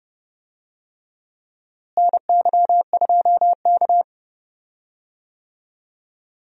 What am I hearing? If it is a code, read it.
Morse code, heard as a keyed tone, about 30 wpm, 700 Hz. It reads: NY2K
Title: USA Call Signs - 4 Characters - Repeated 3x - 30wpm